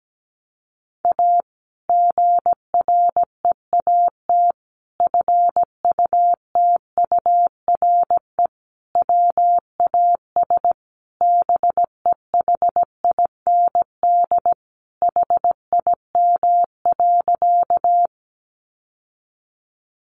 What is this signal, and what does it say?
Morse code, heard as a keyed tone, about 17 wpm, 700 Hz. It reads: A GREAT FUTURE WAS BEHIND HIM.
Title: Sentences from Top 700 Words 17wpm